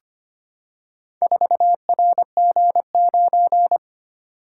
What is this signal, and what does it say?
Morse code, heard as a keyed tone, about 25 wpm, 700 Hz. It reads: 4RG9
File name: practice / 25wpm / Four Letter-Number Combinations - 25wpm